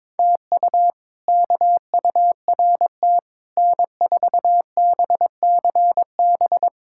Morse code, heard as a keyed tone, about 22 wpm, 700 Hz. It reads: TU KURT N4BCB
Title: K1USN SST QSO Practice - 22wpm